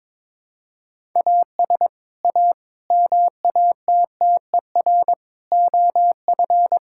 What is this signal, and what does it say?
Morse code, heard as a keyed tone, about 22 wpm, 700 Hz. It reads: AS A MATTER OF
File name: practice / 22wpm / Build A Phrase - 5-Word Phrases - 22wpm